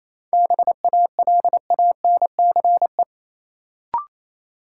Morse code, heard as a keyed tone, about 27 wpm, 700 Hz. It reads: BALANCE
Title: Sentences from Top 900 Words - New Word Drill - 28wpm